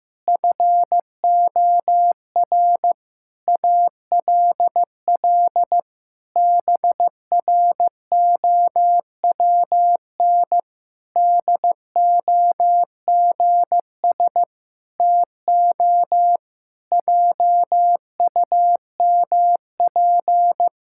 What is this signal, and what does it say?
Morse code, heard as a keyed tone, about 15 wpm, 700 Hz. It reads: FOR ALL BROWN DOGS TO JUMP
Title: Pangram Challenge v2 - 15wpm